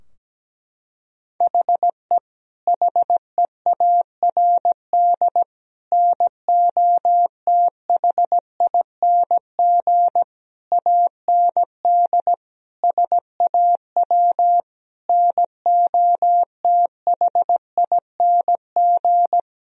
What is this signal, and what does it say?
Morse code, heard as a keyed tone, about 17 wpm, 700 Hz. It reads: HE HEARD NOTHING AND SAW NOTHING
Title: Sentences from Top 1000 Words - Review Part 3 - 17wpm